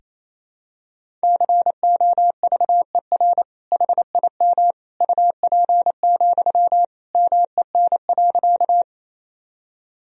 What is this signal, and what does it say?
Morse code, heard as a keyed tone, about 28 wpm, 700 Hz. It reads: COVER HIM UP, MEN.
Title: Sentences from Top 1000 Words - Review Part 1 - 28wpm